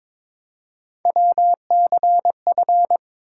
Morse code, heard as a keyed tone, about 22 wpm, 700 Hz. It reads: WCF
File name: ARRL Sections v2 - 22wpm